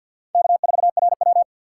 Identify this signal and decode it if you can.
Morse code, heard as a keyed tone, about 53 wpm, 700 Hz. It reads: K4RW
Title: USA Call Signs - 4x Character Spacing - 50wpm